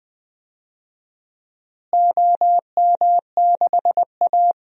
Morse code, heard as a keyed tone, about 20 wpm, 700 Hz. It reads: OM6A
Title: DXCC-100 Call Signs - 4 Characters - Repeated 3x - 20wpm